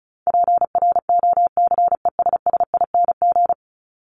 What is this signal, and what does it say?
Morse code, heard as a keyed tone, about 34 wpm, 700 Hz. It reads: PROCESSING
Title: ING Suffix - Top 100 Words with Base Words 35wpm